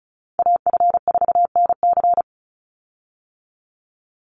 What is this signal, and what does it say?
Morse code, heard as a keyed tone, about 34 wpm, 700 Hz. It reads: AF4NC
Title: Call Signs - 35wpm